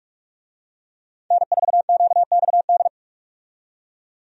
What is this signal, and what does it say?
Morse code, heard as a keyed tone, about 46 wpm, 700 Hz. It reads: N4QXD
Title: USA Call Signs - 5 Characters - 45wpm